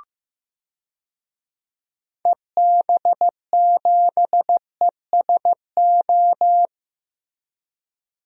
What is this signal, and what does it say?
Morse code, heard as a keyed tone, about 15 wpm, 700 Hz. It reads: EB7ESO
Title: DXCC-100 Call Signs - 6 Characters - Repeated 2x - 15wpm